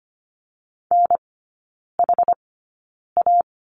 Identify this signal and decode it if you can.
Morse code, heard as a keyed tone, about 25 wpm, spaced as for 9 wpm, 700 Hz. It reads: NHA
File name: course / 25:10wpm / 100 Lesson 17e - Letter u - Three Characters 25wpm